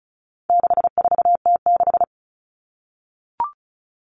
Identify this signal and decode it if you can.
Morse code, heard as a keyed tone, about 34 wpm, 700 Hz. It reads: 64T6
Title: Common Cut Numbers - 4-Digit 35wpm